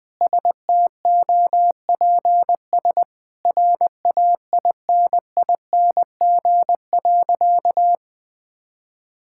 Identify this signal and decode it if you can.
Morse code, heard as a keyed tone, about 21 wpm, 700 Hz. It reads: STOPS RAINING.
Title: Sentences with 2-Word Phrases - 20wpm